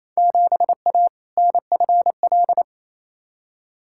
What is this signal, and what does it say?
Morse code, heard as a keyed tone, about 28 wpm, 700 Hz. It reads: 7A NFL
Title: Field Day Exchanges Encore - 28wpm